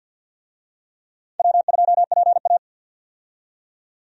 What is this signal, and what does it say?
Morse code, heard as a keyed tone, about 46 wpm, 700 Hz. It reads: W2PA
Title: USA Call Signs - 4 Characters - 50wpm